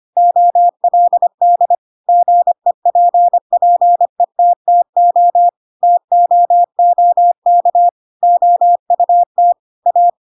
Morse code, heard as a keyed tone, about 25 wpm, 700 Hz. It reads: OLD GEPPETTO TOOK OUT A
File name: Pinocchio / Pinocchio ch7 25 25 ik1hge